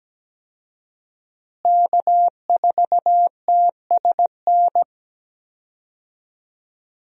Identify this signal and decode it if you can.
Morse code, heard as a keyed tone, about 17 wpm, 700 Hz. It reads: K4TSN
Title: USA Call Signs - 5 Characters - Repeated 2x - 17wpm